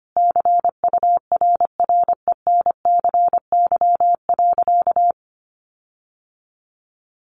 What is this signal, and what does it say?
Morse code, heard as a keyed tone, about 25 wpm, 700 Hz. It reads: CURRENCY.
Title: Sentences from Top 1500 Words 25wpm